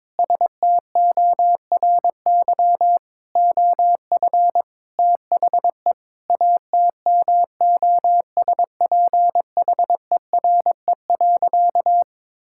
Morse code, heard as a keyed tone, about 22 wpm, 700 Hz. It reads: STORY OF THE ATMOSPHERE.
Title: Sentences from Top 1600 Words 22wpm